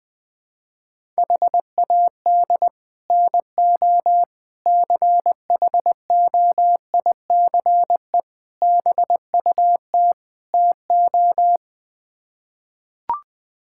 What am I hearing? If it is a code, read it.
Morse code, heard as a keyed tone, about 20 wpm, 700 Hz. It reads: HAD NO CHOICE BUT TO
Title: Build A Phrase - 5-Word Phrases - 20wpm